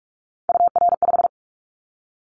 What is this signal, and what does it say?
Morse code, heard as a keyed tone, about 46 wpm, 700 Hz. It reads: UR5
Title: Callsign Prefixes for Top 25 DXCC - 45wpm